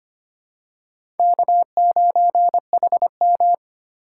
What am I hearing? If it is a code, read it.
Morse code, heard as a keyed tone, about 25 wpm, 700 Hz. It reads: K9HM
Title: USA Call Signs - 4 Characters - 2x Character Spacing - 25wpm